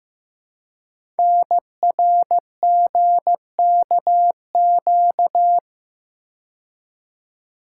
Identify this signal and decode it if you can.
Morse code, heard as a keyed tone, about 15 wpm, 700 Hz. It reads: NRGKQ
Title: Code Groups - 5 Letters - 15wpm